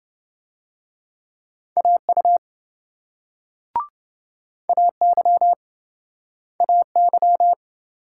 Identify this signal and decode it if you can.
Morse code, heard as a keyed tone, about 27 wpm, 700 Hz. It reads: AU AY AY